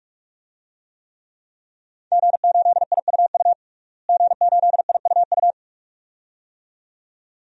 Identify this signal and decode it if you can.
Morse code, heard as a keyed tone, about 41 wpm, 700 Hz. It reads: G8IUU G8IUU
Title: DXCC-100 Call Signs - 5 Characters - Repeated 2x - 45wpm